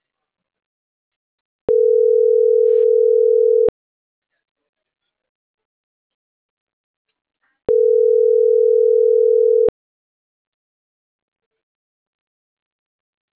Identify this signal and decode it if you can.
Ringback tone (North America), 440 + 480 Hz.